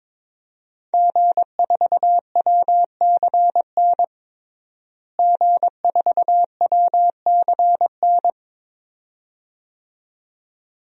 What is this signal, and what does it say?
Morse code, heard as a keyed tone, about 22 wpm, 700 Hz. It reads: G4WCN G4WCN
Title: DXCC-100 Call Signs - 5 Characters - Repeated 2x - 22wpm